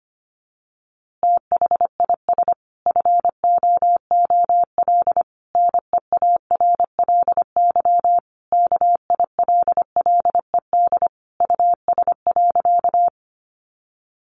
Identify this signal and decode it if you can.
Morse code, heard as a keyed tone, about 25 wpm, 700 Hz. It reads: THIS FOOL NEARLY KILLED US.
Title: A Sound Of Thunder 25wpm